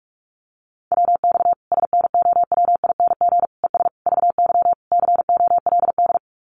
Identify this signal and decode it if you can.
Morse code, heard as a keyed tone, about 44 wpm, 700 Hz. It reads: WX SNOWING ES VY COLD
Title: Common QSO Phrases 45wpm